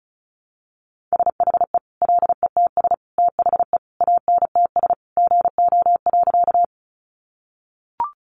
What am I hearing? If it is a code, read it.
Morse code, heard as a keyed tone, about 34 wpm, 700 Hz. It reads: SHE LETS THE ANTS GO.
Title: Easy Phrases 35wpm